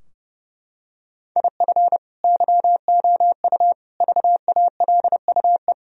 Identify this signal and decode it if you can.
Morse code, heard as a keyed tone, about 30 wpm, 700 Hz. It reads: IF YOU VALUE
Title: Sentences from Top 1000 Words - Review Part 3 - 30wpm